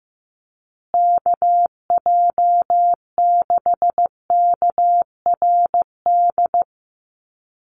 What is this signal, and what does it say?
Morse code, heard as a keyed tone, about 15 wpm, 700 Hz. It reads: KJ6KRD
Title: USA Call Signs Repeated 2x 15wpm